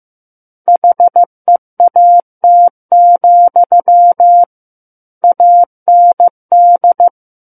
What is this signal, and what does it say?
Morse code, heard as a keyed tone, about 15 wpm, 700 Hz. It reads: HEAT, AND
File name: Chapter 6 - 15 WPM